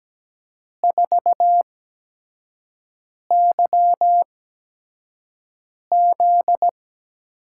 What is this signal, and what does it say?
Morse code, heard as a keyed tone, about 17 wpm, spaced as for 5 wpm, 700 Hz. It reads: 4YZ
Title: USA Call Signs - 5 Characters - 8x Character Spacing - 17wpm